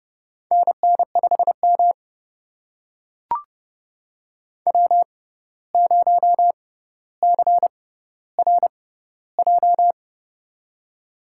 Morse code, heard as a keyed tone, about 30 wpm, spaced as for 14 wpm, 700 Hz. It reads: NN5M W0CRJ
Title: USA Call Signs - 6x Character Spacing - 30wpm